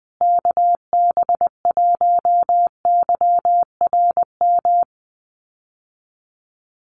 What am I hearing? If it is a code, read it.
Morse code, heard as a keyed tone, about 20 wpm, 700 Hz. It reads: KB1YRM
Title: USA Call Signs Repeated 2x 20wpm